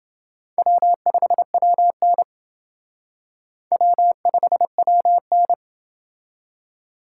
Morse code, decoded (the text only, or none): W5WN W5WN